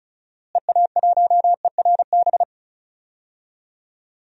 Morse code, heard as a keyed tone, about 34 wpm, 700 Hz. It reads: EA1ERB
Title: DXCC-100 Call Signs - 6 Characters - Repeated 3x - 35wpm